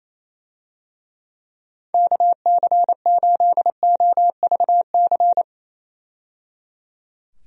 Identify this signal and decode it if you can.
Morse code, heard as a keyed tone, about 28 wpm, 700 Hz. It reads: KC8OVC
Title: USA Call Signs - 6 Characters - 28wpm